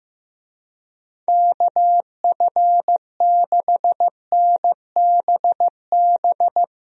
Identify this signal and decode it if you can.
Morse code, heard as a keyed tone, about 15 wpm, 700 Hz. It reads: KF6NBB
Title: USA Call Signs - 6 Characters - Repeated 3x - 15wpm